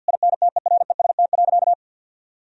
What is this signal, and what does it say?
Morse code, heard as a keyed tone, about 50 wpm, 700 Hz. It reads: INTEREST.
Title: Sentences from Top 1000 Words - Review Part 4 - 50wpm